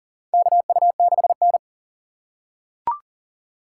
Morse code, heard as a keyed tone, about 40 wpm, 700 Hz. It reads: KU6N
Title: USA Call Signs - 4 Characters - 6x Character Spacing - 40wpm